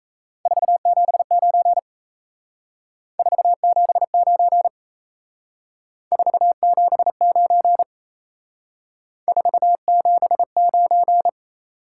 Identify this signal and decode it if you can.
Morse code, heard as a keyed tone, about 34 wpm, 700 Hz. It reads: 479 479 479 479
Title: Speed-Racing Top 180 QSO Elements 28wpm